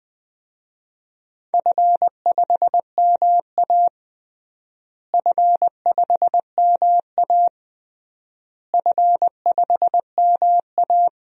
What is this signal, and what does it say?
Morse code, heard as a keyed tone, about 21 wpm, 700 Hz. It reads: F5MA F5MA F5MA
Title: DXCC-100 Call Signs - 4 Characters - Repeated 3x - 20wpm